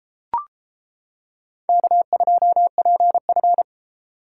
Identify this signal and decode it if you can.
Morse code, heard as a keyed tone, about 34 wpm, 700 Hz. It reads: K2PF